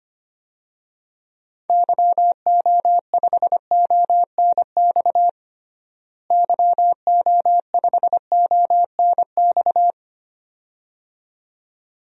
Morse code, heard as a keyed tone, about 25 wpm, 700 Hz. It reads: YO5ONX YO5ONX
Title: DXCC-100 Call Signs - 6 Characters - Repeated 2x - 25wpm